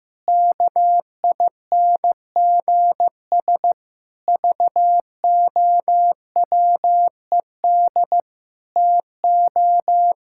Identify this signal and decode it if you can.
Morse code, heard as a keyed tone, about 15 wpm, 700 Hz. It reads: KINGS VOWED TO
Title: Pangram Challenge v2 - 15wpm